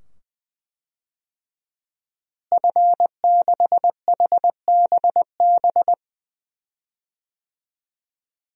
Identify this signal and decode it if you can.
Morse code, heard as a keyed tone, about 21 wpm, 700 Hz. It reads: F6HBB
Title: DXCC-100 Call Signs - 5 Characters - Repeated 2x - 20wpm